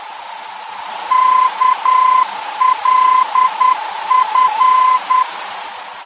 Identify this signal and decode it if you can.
Morse code, heard as a keyed tone, about 10 wpm, 1020 Hz. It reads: KLF
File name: germany / mx / KLF-D-VOR